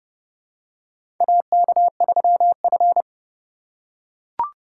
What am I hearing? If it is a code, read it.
Morse code, heard as a keyed tone, about 30 wpm, 700 Hz. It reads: AK3F
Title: USA Call Signs - 4 Characters - 2x Character Spacing - 30wpm